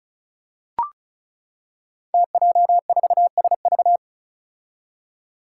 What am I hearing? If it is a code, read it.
Morse code, heard as a keyed tone, about 36 wpm, 700 Hz. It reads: TJ4SV
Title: Five Challenging Letter-Number Combinations 35wpm